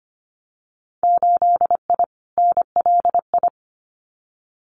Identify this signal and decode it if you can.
Morse code, heard as a keyed tone, about 25 wpm, 700 Hz. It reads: 8I NLI
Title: Winter Field Day Exchanges Encore 25wpm